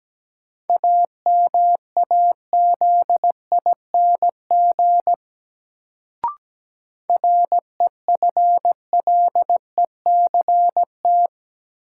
Morse code, heard as a keyed tone, about 17 wpm, 700 Hz. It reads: AMAZING REFLECT
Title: Sentences from Top 1500 Words - New Word Drill - 17wpm